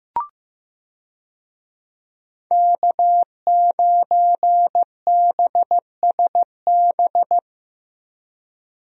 Morse code, heard as a keyed tone, about 15 wpm, 700 Hz. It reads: K9BSB
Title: USA Call Signs - 5 Characters - Repeated 3x - 15wpm